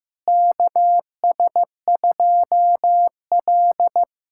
Morse code, heard as a keyed tone, about 15 wpm, 700 Hz. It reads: KS2L
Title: USA Callsign DE Callsign - 4 Characters - 15wpm